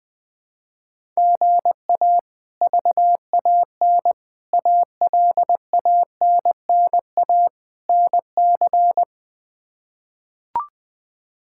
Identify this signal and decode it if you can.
Morse code, heard as a keyed tone, about 20 wpm, 700 Hz. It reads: GA VAN ALANNA NC
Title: K1USN SST QSO Practice - 20wpm